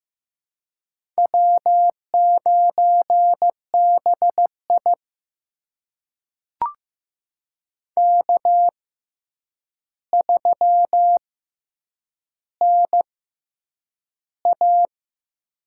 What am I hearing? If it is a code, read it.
Morse code, heard as a keyed tone, about 15 wpm, spaced as for 8 wpm, 700 Hz. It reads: W9BI K3NA
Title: USA Call Signs - 6x Character Spacing - 15wpm